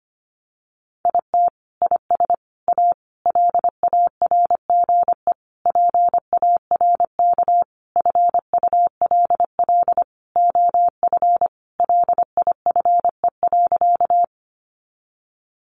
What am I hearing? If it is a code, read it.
Morse code, heard as a keyed tone, about 25 wpm, 700 Hz. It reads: IT IS A LARGE PARK FULL OF LIFE.
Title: Sentences from Top 1300 Words 25wpm